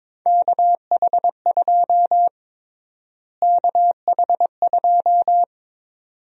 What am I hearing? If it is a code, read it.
Morse code, heard as a keyed tone, about 22 wpm, 700 Hz. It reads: KH2 KH2
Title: USA Call Sign Prefixes - Repeated 2x - 22wpm